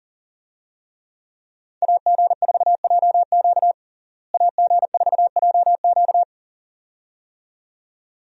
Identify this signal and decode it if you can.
Morse code, heard as a keyed tone, about 37 wpm, 700 Hz. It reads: AG4JQ AG4JQ
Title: USA Call Signs - 5 Characters - Repeated 2x - 40wpm